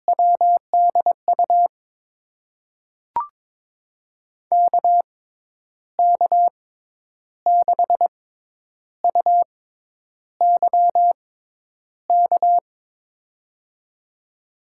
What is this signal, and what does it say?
Morse code, heard as a keyed tone, about 22 wpm, spaced as for 9 wpm, 700 Hz. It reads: WDU KK6UYK
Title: USA Call Signs - 6 Characters - 6x Character Spacing - 22wpm